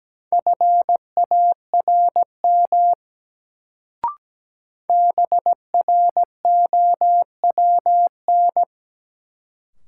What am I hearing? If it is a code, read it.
Morse code, heard as a keyed tone, about 17 wpm, 700 Hz. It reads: FARM BROWN
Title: Sentences from Top 1200 Words - New Word Drill Alt - 17wpm